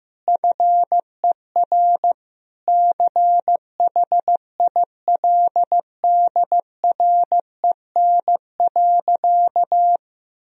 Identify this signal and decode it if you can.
Morse code, heard as a keyed tone, about 15 wpm, 700 Hz. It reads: FER CHILDREN.
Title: Sentences from Top 400 Words and Abbreviations 15wpm